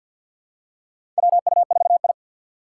Morse code, heard as a keyed tone, about 52 wpm, 700 Hz. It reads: WU4I
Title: USA Call Signs - 4 Characters - 2x Character Spacing - 50wpm